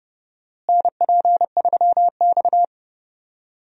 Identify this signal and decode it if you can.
Morse code, heard as a keyed tone, about 30 wpm, 700 Hz. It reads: NP3X
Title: USA Call Signs - 4 Characters - Repeated 2x - 30wpm